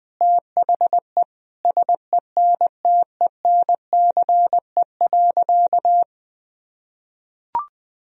Morse code, heard as a keyed tone, about 20 wpm, 700 Hz. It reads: THE SENTENCE.
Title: Sentences from Top 1000 Words - Review Part 1 - 20wpm